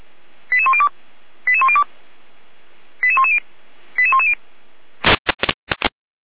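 Five-tone selective call (ZVEI), heard as quick runs of five tones, tones of about 70 ms. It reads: 88102 88102 88110 88110
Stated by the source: ZVEI five-tone call received by SDR, archived by feuerwehr-orlamuende.org (Orlamünde fire brigade).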